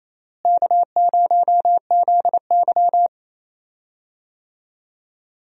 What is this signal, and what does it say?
Morse code, heard as a keyed tone, about 27 wpm, 700 Hz. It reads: K0ZY
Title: USA Call Signs - 4 Characters - Repeated 2x - 28wpm